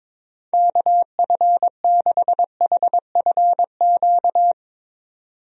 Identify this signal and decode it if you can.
Morse code, heard as a keyed tone, about 22 wpm, 700 Hz. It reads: KF6HFQ
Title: USA Call Signs - 6 Characters - 22wpm